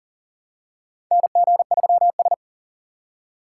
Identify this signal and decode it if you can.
Morse code, heard as a keyed tone, about 39 wpm, 700 Hz. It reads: NG3S
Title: USA Call Signs - 2x Character Spacing - 40wpm